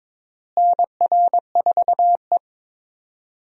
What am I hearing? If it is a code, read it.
Morse code, heard as a keyed tone, about 22 wpm, 700 Hz. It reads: NR4E
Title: USA Call Signs - 4 Characters - Repeated 3x - 22wpm